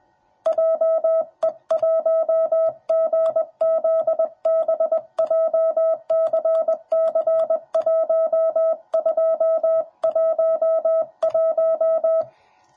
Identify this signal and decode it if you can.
Morse code, heard as a keyed tone, about 21 wpm, 650 Hz. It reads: JE1GZB JCC1211
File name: jh8cbh / 2021cw / cw10b